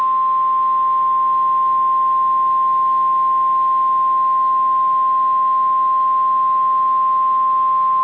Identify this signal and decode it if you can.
Emergency Alert System (SAME) tones (the 1050 Hz attention tone). It is heard from 0.5 to 7.7 s.